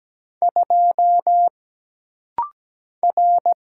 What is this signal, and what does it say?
Morse code, heard as a keyed tone, about 17 wpm, 700 Hz. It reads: 2 R
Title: Single Letter-Number - Rapid-Fire 17wpm